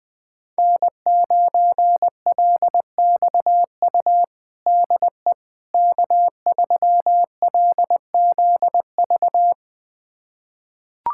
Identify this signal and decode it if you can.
Morse code, heard as a keyed tone, about 21 wpm, 700 Hz. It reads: N9LXU DE K3LZV
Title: USA Callsign DE Callsign - 5 Characters - 20wpm